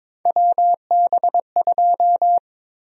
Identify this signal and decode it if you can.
Morse code, heard as a keyed tone, about 22 wpm, 700 Hz. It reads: WB2
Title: USA Call Sign Prefixes - 22wpm